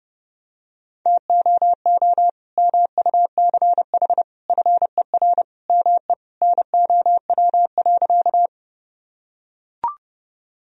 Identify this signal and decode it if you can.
Morse code, heard as a keyed tone, about 30 wpm, 700 Hz. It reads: TOO MUCH FER ME NOW.
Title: Sentences from Top 200 Words and Abbreviations 30wpm